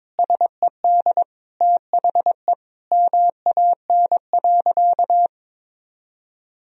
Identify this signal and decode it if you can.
Morse code, heard as a keyed tone, about 22 wpm, 700 Hz. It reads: SED THE MAN.
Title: Sentences from Top 400 Words and Abbreviations 22wpm